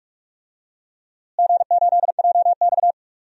Morse code, heard as a keyed tone, about 46 wpm, 700 Hz. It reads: G8JX
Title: DXCC-100 Call Signs - 4 Characters - Repeated 2x - 45wpm